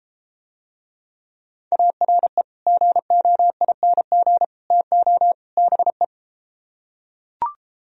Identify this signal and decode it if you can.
Morse code, heard as a keyed tone, about 33 wpm, 700 Hz. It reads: ARE GOING TO BE